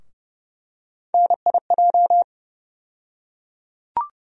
Morse code, heard as a keyed tone, about 30 wpm, 700 Hz. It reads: NIJ
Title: USA Call Sign Postfixes - 3 Letters 30wpm